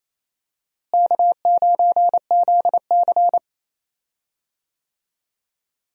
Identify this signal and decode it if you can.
Morse code, heard as a keyed tone, about 28 wpm, 700 Hz. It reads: K9ZC
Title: USA Call Signs - 4 Characters - Repeated 3x - 28wpm